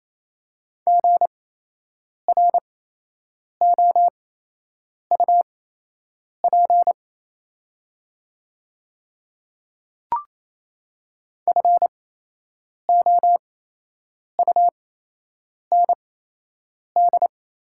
Morse code, heard as a keyed tone, about 28 wpm, spaced as for 8 wpm, 700 Hz. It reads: GROUP FOUND
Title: 8x Character Spacing - Top 500 Words - 28wpm